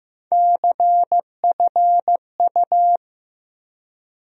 Morse code, heard as a keyed tone, about 15 wpm, 700 Hz. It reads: CFU
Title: USA Call Sign Postfixes - 3 Letters 15wpm